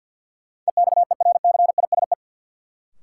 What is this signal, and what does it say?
Morse code, heard as a keyed tone, about 47 wpm, 700 Hz. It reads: EXERCISE
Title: Sentences from Top 1200 Words - New Word Drill Alt - 50wpm